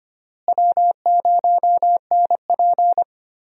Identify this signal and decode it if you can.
Morse code, heard as a keyed tone, about 25 wpm, 700 Hz. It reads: W0NP
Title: USA Call Signs - 4 Characters - Repeated 2x - 25wpm